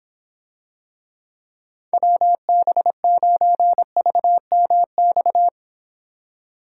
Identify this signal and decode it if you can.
Morse code, heard as a keyed tone, about 26 wpm, 700 Hz. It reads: WB9VMX